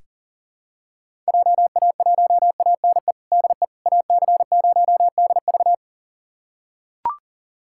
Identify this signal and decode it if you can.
Morse code, heard as a keyed tone, about 40 wpm, 700 Hz. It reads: JA1ANE DE AC0DV
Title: USA Callsign DE Callsign 40wpm